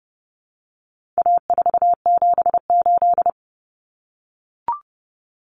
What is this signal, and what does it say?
Morse code, heard as a keyed tone, about 30 wpm, 700 Hz. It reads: A478
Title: Common Cut Numbers - 4-Digit 30wpm